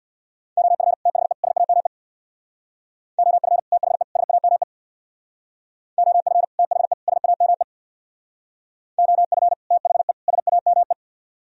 Morse code, heard as a keyed tone, about 61 wpm, 700 Hz. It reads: OF THE SAME OF THE SAME OF THE SAME OF THE SAME
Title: Speed-Racing - Sets of 3 Words 50wpm